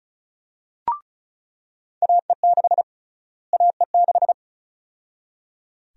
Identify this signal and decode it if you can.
Morse code, heard as a keyed tone, about 34 wpm, 700 Hz. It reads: AE6 AE6
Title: USA Call Sign Prefixes - Repeated 2x - 35wpm